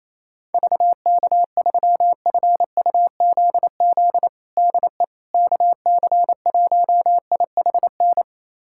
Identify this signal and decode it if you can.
Morse code, heard as a keyed tone, about 28 wpm, 700 Hz. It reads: VK3FUZZ DE KC1IHN
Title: USA Callsign DE Callsign 28wpm